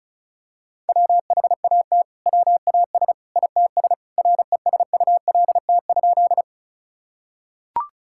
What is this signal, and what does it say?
Morse code, heard as a keyed tone, about 36 wpm, 700 Hz. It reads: WHAT WAS ITS RESULT?
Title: Sentences from Top 1000 Words - Review Part 5 - 35wpm